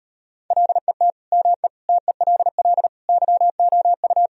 Morse code, heard as a keyed tone, about 38 wpm, 700 Hz. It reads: LET ME TELL YOU